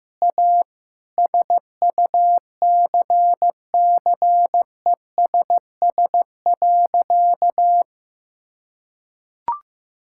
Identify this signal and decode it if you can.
Morse code, heard as a keyed tone, about 15 wpm, 700 Hz. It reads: A SUCCESS.
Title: Sentences from Top 1000 Words - Review Part 6 - 15wpm